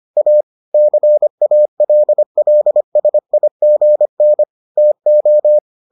Morse code, heard as a keyed tone, about 25 wpm, 600 Hz. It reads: A CALLSIGN TO